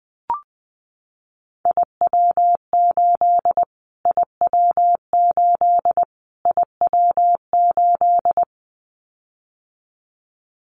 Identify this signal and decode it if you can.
Morse code, heard as a keyed tone, about 20 wpm, 700 Hz. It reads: IW8 IW8 IW8
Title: Callsign Prefixes for Top 25 DXCC - Repeated 3x - 20wpm